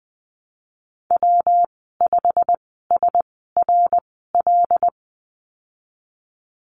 Morse code, heard as a keyed tone, about 20 wpm, spaced as for 15 wpm, 700 Hz. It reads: W5SRL
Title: 140 Lesson 23g - Letter f - Call Signs 20wpm